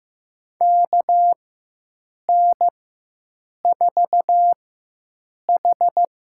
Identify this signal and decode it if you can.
Morse code, heard as a keyed tone, about 15 wpm, spaced as for 7 wpm, 700 Hz. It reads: KN4H
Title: USA Call Signs - 4x Character Spacing - 15wpm